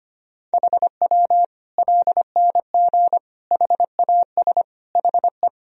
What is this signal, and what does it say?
Morse code, heard as a keyed tone, about 25 wpm, 700 Hz. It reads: HW LNG HAS HE
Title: Sentences from Top 100 Words and Abbreviations 25wpm